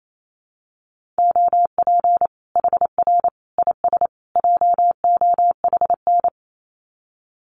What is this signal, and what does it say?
Morse code, heard as a keyed tone, about 28 wpm, 700 Hz. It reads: OP HR IS JOHN
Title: Common QSO Phrases 28wpm